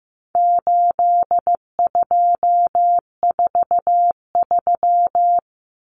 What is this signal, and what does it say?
Morse code, heard as a keyed tone, about 15 wpm, 700 Hz. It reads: 8243
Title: Common Cut Numbers - 4-Digit 15wpm